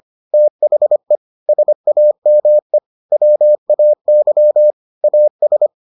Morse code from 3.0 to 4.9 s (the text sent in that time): WAY